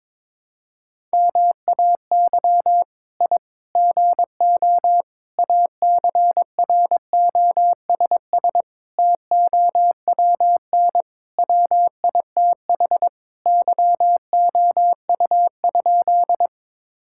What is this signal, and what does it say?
Morse code, heard as a keyed tone, about 22 wpm, 700 Hz. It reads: MAY I GO ACROSS TOWN WITH YOU?
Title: Sentences from Top 1000 Words - Review Part 5 - 22wpm